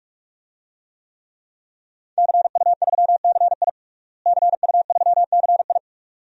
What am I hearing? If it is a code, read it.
Morse code, heard as a keyed tone, about 46 wpm, 700 Hz. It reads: CU3CI CU3CI
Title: DXCC-100 Call Signs - 5 Characters - Repeated 2x - 45wpm